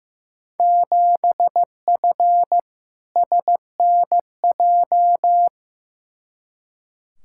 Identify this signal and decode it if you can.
Morse code, heard as a keyed tone, about 15 wpm, 700 Hz. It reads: 7F SNJ
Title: Field Day Exchanges Encore - 15wpm